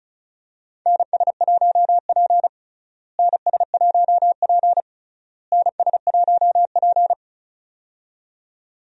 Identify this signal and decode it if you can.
Morse code, heard as a keyed tone, about 34 wpm, 700 Hz. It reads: NS1P NS1P NS1P
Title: USA Call Signs - 4 Characters - Repeated 3x - 35wpm